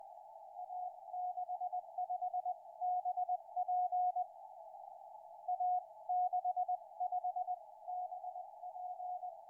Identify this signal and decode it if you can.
Morse code, heard as a keyed tone, about 20 wpm, spaced as for 17 wpm, 720 Hz. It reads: A65BP A65BP
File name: a65bp 30-11-13 1958 160m